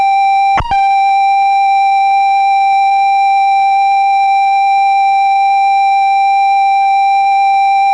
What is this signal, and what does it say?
Computer data on cassette.
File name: Software / Cassettes / Penny Arcade